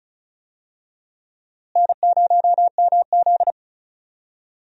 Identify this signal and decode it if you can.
Morse code, heard as a keyed tone, about 36 wpm, 700 Hz. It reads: N0MZ
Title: USA Call Signs - 4 Characters - Repeated 2x - 35wpm